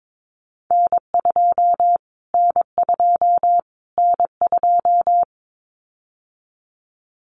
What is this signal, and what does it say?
Morse code, heard as a keyed tone, about 22 wpm, 700 Hz. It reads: N2 N2 N2
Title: Callsign Prefixes for Top 25 DXCC - Repeated 3x - 22wpm